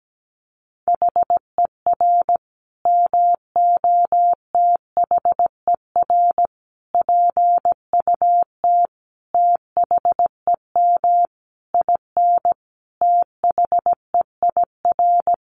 Morse code, heard as a keyed tone, about 17 wpm, 700 Hz. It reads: HER MOTHER PUT THEM IN THEIR
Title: Sentences from Top 800 Words 17wpm